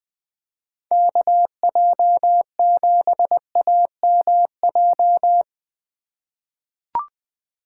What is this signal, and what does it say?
Morse code, heard as a keyed tone, about 20 wpm, 700 Hz. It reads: KJ7AMJ
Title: USA Call Signs - 6 Characters - 3x Character Spacing - 20wpm